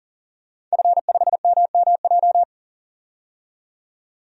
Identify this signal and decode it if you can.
Morse code, heard as a keyed tone, about 40 wpm, 700 Hz. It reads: F5MMJ
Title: DXCC-100 Call Signs - 5 Characters - 40wpm